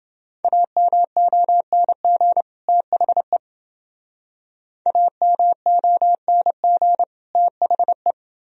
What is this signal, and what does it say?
Morse code, heard as a keyed tone, about 28 wpm, 700 Hz. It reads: AMONG THE AMONG THE